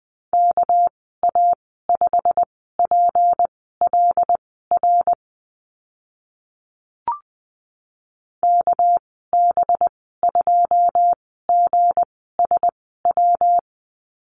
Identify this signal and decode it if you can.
Morse code, heard as a keyed tone, about 20 wpm, spaced as for 14 wpm, 700 Hz. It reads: KA5PLR KB2GSW
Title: 213 Lesson 34g - Number 8 - Call Signs 20wpm